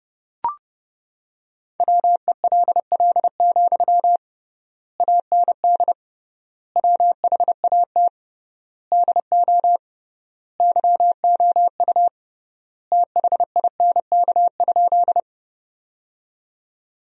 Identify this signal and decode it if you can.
Morse code, heard as a keyed tone, about 30 wpm, 700 Hz. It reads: WELL, AND WHAT DO YOU THINK?
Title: 3x Word-Spacing - Sentences from Top 200 Words - 30wpm